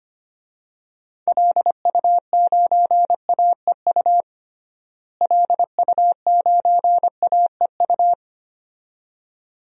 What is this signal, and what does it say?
Morse code, heard as a keyed tone, about 25 wpm, 700 Hz. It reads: LU9AEU LU9AEU
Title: DXCC-100 Call Signs - 6 Characters - Repeated 2x - 25wpm